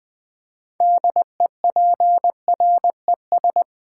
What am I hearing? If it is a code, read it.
Morse code, heard as a keyed tone, about 20 wpm, 700 Hz. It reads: DEPRES
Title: Build A Word - 3 Letters At A Time - 9-Letter Words - 20wpm